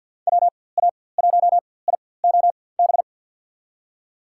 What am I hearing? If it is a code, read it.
Morse code, heard as a keyed tone, about 49 wpm, spaced as for 23 wpm, 700 Hz. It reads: WA1IOB
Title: USA Call Signs - 4x Character Spacing - 50wpm